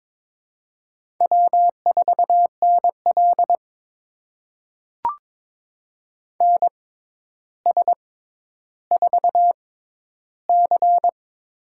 Morse code, heard as a keyed tone, about 21 wpm, spaced as for 11 wpm, 700 Hz. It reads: W4NL NS4C
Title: USA Call Signs - 4 Characters - 6x Character Spacing - 22wpm